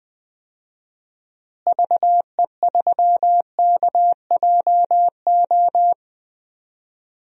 Morse code, heard as a keyed tone, about 20 wpm, 700 Hz. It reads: VE3KJO
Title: DXCC-100 Call Signs - 6 Characters - Repeated 2x - 20wpm